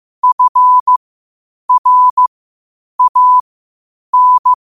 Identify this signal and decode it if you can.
Morse code, heard as a keyed tone, about 15 wpm, spaced as for 8 wpm, 1000 Hz. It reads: FRAN
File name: qotdfiles / 8wpm